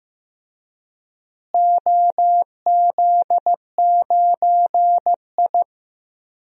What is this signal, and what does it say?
Morse code, heard as a keyed tone, about 15 wpm, 700 Hz. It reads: OZ9I
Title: DXCC-100 Call Signs - 4 Characters - Repeated 2x - 15wpm